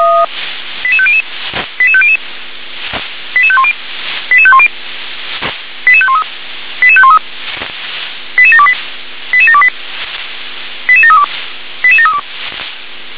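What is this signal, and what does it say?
Five-tone selective call (ZVEI), heard as quick runs of five tones, tones of about 70 ms. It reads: 88500 88500 88510 88510 88513 88513 88728 88728 88732 88732
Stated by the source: Five-tone ZVEI selective call received on SDR.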